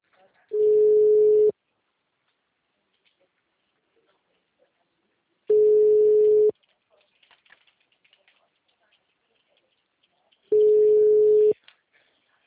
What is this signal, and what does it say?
Ringback tone (Europe, 425 Hz).
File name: out- -2030- - - .12380